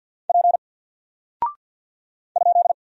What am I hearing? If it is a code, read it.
Morse code, heard as a keyed tone, about 48 wpm, 700 Hz. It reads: P ?
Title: Single Character (Letter:Number:Punctuation:Prosign) 50wpm